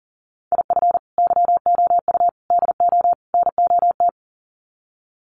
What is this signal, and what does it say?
Morse code, heard as a keyed tone, about 40 wpm, 700 Hz. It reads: IF YOU DO NOT
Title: Sets of 4 Words 40wpm